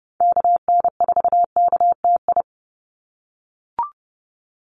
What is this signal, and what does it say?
Morse code, heard as a keyed tone, about 30 wpm, 700 Hz. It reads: KN4KTI
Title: USA Call Signs 30wpm